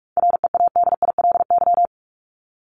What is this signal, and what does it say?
Morse code, heard as a keyed tone, about 44 wpm, 700 Hz. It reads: READILY
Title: LY Suffix - Top 100 Words with Base Words 45wpm